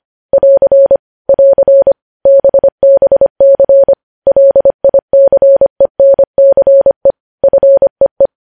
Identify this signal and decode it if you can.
Morse code, heard as a keyed tone, about 25 wpm, 550 Hz. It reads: <AR> <AR> BBC LICENCE FEE
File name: podcast / News headlines-1x-25x25